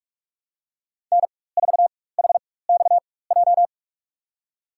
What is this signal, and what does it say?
Morse code, heard as a keyed tone, about 46 wpm, spaced as for 21 wpm, 700 Hz. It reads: N4HXJ
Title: USA Call Signs - 5 Characters - 4x Character Spacing - 45wpm